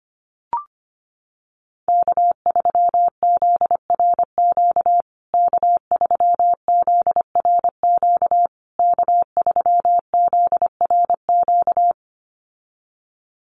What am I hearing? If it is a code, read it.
Morse code, heard as a keyed tone, about 25 wpm, 700 Hz. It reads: K3ZRQ K3ZRQ K3ZRQ
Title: USA Call Signs Repeated 3x 25wpm